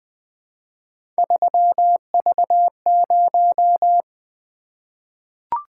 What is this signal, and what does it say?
Morse code, heard as a keyed tone, about 20 wpm, 700 Hz. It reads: 3V0
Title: Three Letter-Number Combinations - 20wpm